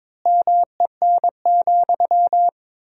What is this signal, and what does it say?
Morse code, heard as a keyed tone, about 22 wpm, 700 Hz. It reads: MEN,
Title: 4x Word-Spacing - Sentences from Top 500 Words - 22wpm